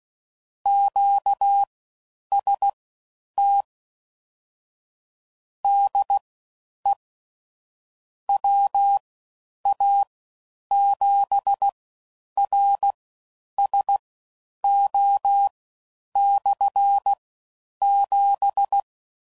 Morse code, heard as a keyed tone, about 16 wpm, spaced as for 9 wpm, 800 Hz. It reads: QST DE WA7RSO/7
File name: Basic Sets / QST de WA7RSO / All Dahs